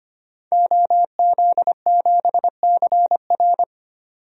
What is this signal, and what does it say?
Morse code, heard as a keyed tone, about 25 wpm, 700 Hz. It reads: OZ7CR
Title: DXCC-100 Call Signs - 5 Characters - 25wpm